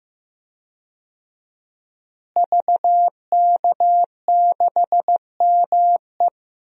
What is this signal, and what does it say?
Morse code, heard as a keyed tone, about 15 wpm, 700 Hz. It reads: VK6ME
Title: DXCC-100 Call Signs - 5 Characters - Repeated 2x - 15wpm